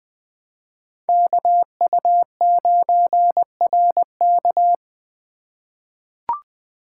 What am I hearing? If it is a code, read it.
Morse code, heard as a keyed tone, about 20 wpm, 700 Hz. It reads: KU9RK
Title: K1USN SST QSO Practice - 20wpm